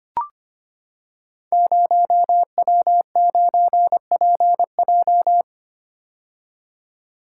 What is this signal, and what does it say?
Morse code, heard as a keyed tone, about 25 wpm, 700 Hz. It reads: 0W9PJ
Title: Code Groups - 5 Letter:Numbers - 25wpm